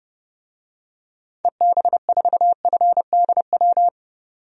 Morse code, heard as a keyed tone, about 29 wpm, 700 Hz. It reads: EB4FDW
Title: DXCC-100 Call Signs - 6 Characters - 30wpm